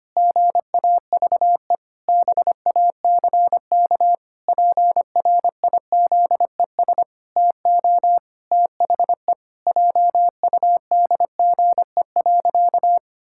Morse code, heard as a keyed tone, about 25 wpm, 700 Hz. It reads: GAVE BACK PRIZES TO THE JUDGE.
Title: Pangram Challenge v2 - 25wpm